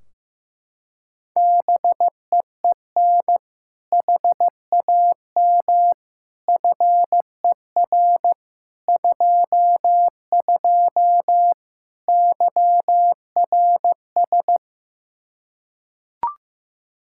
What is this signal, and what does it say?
Morse code, heard as a keyed tone, about 15 wpm, 700 Hz. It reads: BEEN HAM FER 22 YRS
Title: Practice QSO - Exchange 3 - More Info - 15wpm